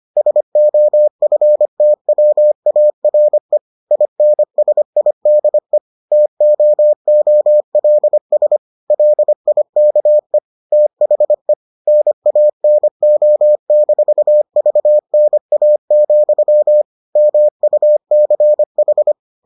Morse code, heard as a keyed tone, about 25 wpm, 600 Hz. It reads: SOFTWARE INSIDE TOOLS LIKE THE NANO-VNA, MUCH